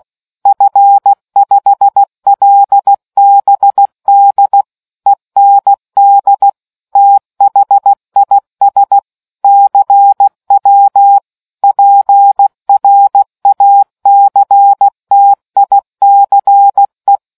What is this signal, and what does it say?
Morse code, heard as a keyed tone, about 16 wpm, 800 Hz. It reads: F5LBD END THIS CW PRACTICE